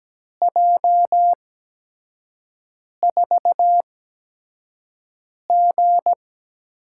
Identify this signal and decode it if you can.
Morse code, heard as a keyed tone, about 17 wpm, spaced as for 5 wpm, 700 Hz. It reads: J4G
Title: USA Call Signs - 4 Characters - 8x Character Spacing - 17wpm